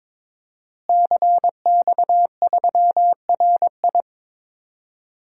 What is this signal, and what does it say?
Morse code, heard as a keyed tone, about 22 wpm, 700 Hz. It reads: CX3RI
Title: DXCC-100 Call Signs - 5 Characters - Repeated 3x - 22wpm